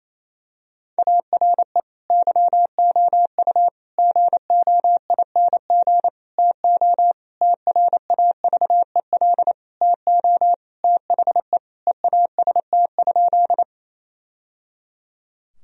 Morse code, heard as a keyed tone, about 28 wpm, 700 Hz. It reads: ARE YOU GOING TO TRAVEL TO THE EAST?
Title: Sentences from Top 1000 Words - Review Part 3 - 28wpm